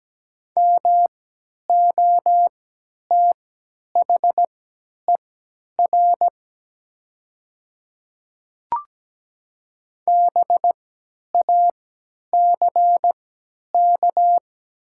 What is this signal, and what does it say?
Morse code, heard as a keyed tone, about 17 wpm, spaced as for 10 wpm, 700 Hz. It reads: MOTHER BACK
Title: 3x Character Spacing - Top 500 Words - 17wpm